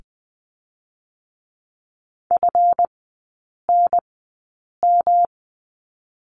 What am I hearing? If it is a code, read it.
Morse code, heard as a keyed tone, about 20 wpm, 700 Hz. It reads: F N M
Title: 145 Lesson 24e - Letter y - Three Characters 20wpm